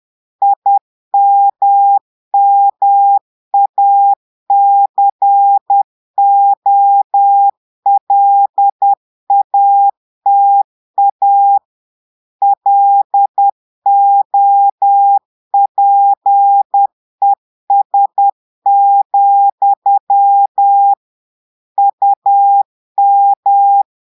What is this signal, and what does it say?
Morse code, heard as a keyed tone, about 10 wpm, 800 Hz. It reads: IMMACOLATA LOPES, UM
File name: Portugais - texte en clair 1 - vitesse - 10 mots minute